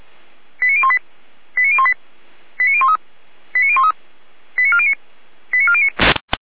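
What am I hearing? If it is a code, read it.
Five-tone selective call (ZVEI), heard as quick runs of five tones, tones of about 70 ms. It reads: 89018 89018 89013 89013 89409 89409
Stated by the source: ZVEI five-tone call received by SDR, recorded by the Orlamünde fire brigade (feuerwehr-orlamuende).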